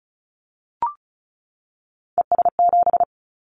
Morse code, heard as a keyed tone, about 36 wpm, 700 Hz. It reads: ES7
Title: Callsign Prefixes for Top 50 DXCC - 35wpm